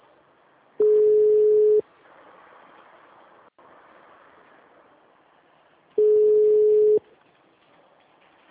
Ringback tone (Europe, 425 Hz).
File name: out- -2009- - - .18455